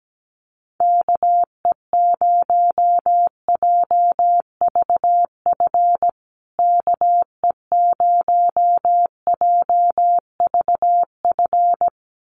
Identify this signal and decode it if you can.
Morse code, heard as a keyed tone, about 17 wpm, 700 Hz. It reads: KE0JVF KE0JVF
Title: USA Call Signs Repeated 2x 17wpm